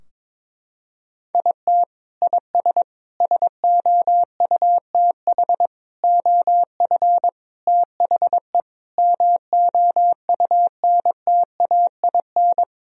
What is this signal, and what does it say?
Morse code, heard as a keyed tone, about 22 wpm, 700 Hz. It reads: IT IS SOUTH OF THE MOUNTAIN
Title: Sentences from Top 1000 Words - Review Part 4 - 22wpm